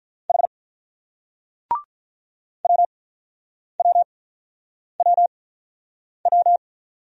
Morse code, heard as a keyed tone, about 47 wpm, spaced as for 9 wpm, 700 Hz. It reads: H WWWW